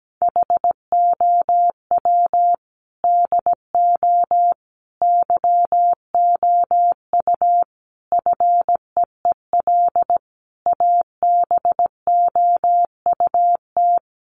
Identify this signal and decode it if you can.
Morse code, heard as a keyed tone, about 17 wpm, 700 Hz. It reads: HOW DO YOU FEEL ABOUT
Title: Sets of 5 Words 17wpm